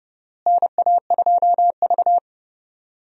Morse code, heard as a keyed tone, about 30 wpm, 700 Hz. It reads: NA2V
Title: USA Call Signs - 4 Characters - Repeated 3x - 30wpm